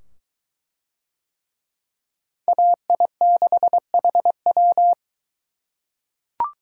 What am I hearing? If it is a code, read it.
Morse code, heard as a keyed tone, about 22 wpm, 700 Hz. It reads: AI6HW